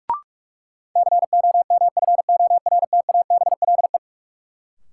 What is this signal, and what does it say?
Morse code, heard as a keyed tone, about 42 wpm, 700 Hz. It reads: COMFORTABLE
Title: Sentences from Top 1000 Words - Review Part 7 - New Word Drill - 45wpm